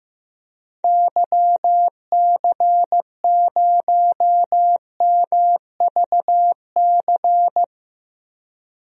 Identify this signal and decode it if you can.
Morse code, heard as a keyed tone, about 15 wpm, 700 Hz. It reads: YC0MVC
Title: DXCC-100 Call Signs - 6 Characters - 15wpm